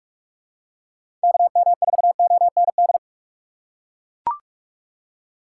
Morse code, heard as a keyed tone, about 44 wpm, 700 Hz. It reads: KM4OND
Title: USA Call Signs - 4x Character Spacing - 45wpm